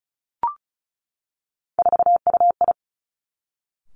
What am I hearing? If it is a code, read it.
Morse code, heard as a keyed tone, about 36 wpm, 700 Hz. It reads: 4UI
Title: Three Challenging Letter-Number Combinations 35wpm